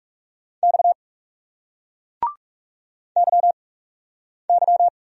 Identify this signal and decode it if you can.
Morse code, heard as a keyed tone, about 41 wpm, 700 Hz. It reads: X Y Y